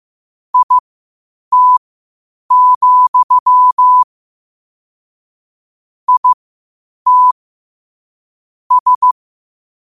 Morse code, heard as a keyed tone, about 15 wpm, spaced as for 8 wpm, 1000 Hz. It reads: IT, IT S
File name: qotdfiles / 8wpm